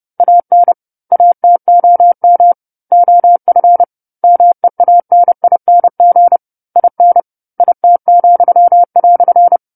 Morse code, heard as a keyed tone, about 30 wpm, 700 Hz. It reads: AN ATOM OF MEANING IN IT,"
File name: 10-Chapter-10 30 WPM